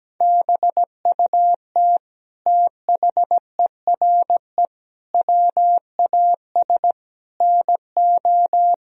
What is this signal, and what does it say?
Morse code, heard as a keyed tone, about 17 wpm, 700 Hz. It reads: BUT THERE WAS NO
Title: Sentences from Top 1000 Words - Review Part 2 - 17wpm